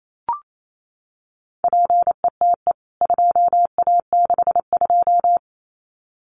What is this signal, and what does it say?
Morse code, heard as a keyed tone, about 28 wpm, 700 Hz. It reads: PETE 2A62
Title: CWT Exchange 28wpm